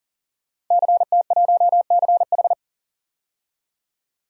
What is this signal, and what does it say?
Morse code, heard as a keyed tone, about 40 wpm, 700 Hz. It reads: CT1CH
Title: DXCC-100 Call Signs - 5 Characters - 40wpm